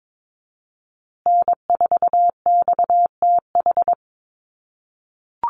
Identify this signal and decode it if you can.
Morse code, heard as a keyed tone, about 22 wpm, 700 Hz. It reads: N4XTH
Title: USA Call Signs Repeated 3x 22wpm